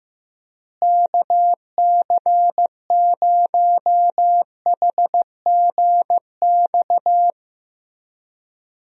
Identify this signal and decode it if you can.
Morse code, heard as a keyed tone, about 15 wpm, 700 Hz. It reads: KC0HGX
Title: USA Call Signs - 6 Characters - Repeated 3x - 15wpm